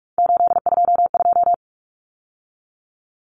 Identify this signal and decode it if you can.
Morse code, heard as a keyed tone, about 44 wpm, 700 Hz. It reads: 822
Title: Numbers - 3-Digit 45wpm